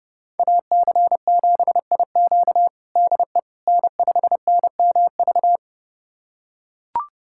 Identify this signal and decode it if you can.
Morse code, heard as a keyed tone, about 30 wpm, 700 Hz. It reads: AC7IQ DE N5NMV
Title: USA Callsign DE Callsign - 5 Characters - 30wpm